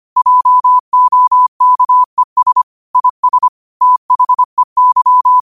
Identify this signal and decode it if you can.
Morse code, heard as a keyed tone, about 25 wpm, 1000 Hz. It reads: JOKES IS THEY
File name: qotdfiles / 25wpm